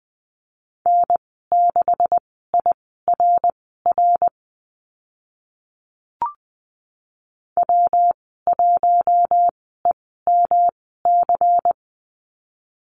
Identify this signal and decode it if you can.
Morse code, heard as a keyed tone, about 20 wpm, spaced as for 14 wpm, 700 Hz. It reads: N6IRR W1EMC
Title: 140 Lesson 23g - Letter f - Call Signs 20wpm